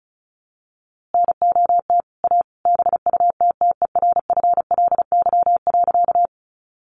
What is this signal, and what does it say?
Morse code, heard as a keyed tone, about 34 wpm, 700 Hz. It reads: NOT A BUTTERFLY.
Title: A Sound Of Thunder 35wpm